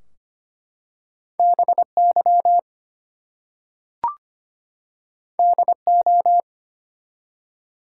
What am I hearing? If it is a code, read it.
Morse code, heard as a keyed tone, about 25 wpm, 700 Hz. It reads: BY DO
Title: Two Letter Words - 25wpm